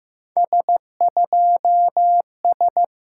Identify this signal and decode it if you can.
Morse code, heard as a keyed tone, about 16 wpm, 700 Hz. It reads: S2S
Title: SOTA QSO Practice 15wpm